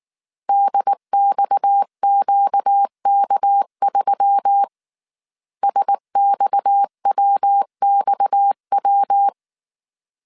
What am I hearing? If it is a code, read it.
Morse code, heard as a keyed tone, about 19 wpm, 800 Hz. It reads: DXQK3 SXWXW
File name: CW / DET / 1 6 1500C S A 18wpm 2255